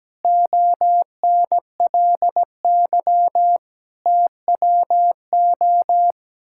Morse code, heard as a keyed tone, about 17 wpm, 700 Hz. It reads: ONLY TWO
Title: Sentences from Top 1000 Words - Review Part 6 - 17wpm